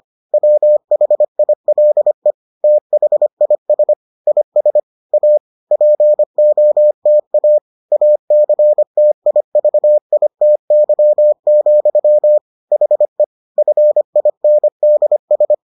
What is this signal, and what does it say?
Morse code, heard as a keyed tone, about 25 wpm, 600 Hz. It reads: WHILE THIS IS A POTA ACTIVITY, HE FINDS